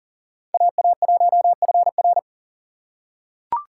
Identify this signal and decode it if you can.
Morse code, heard as a keyed tone, about 40 wpm, 700 Hz. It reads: AA1FR
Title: USA Call Signs - 5 Characters - 4x Character Spacing - 40wpm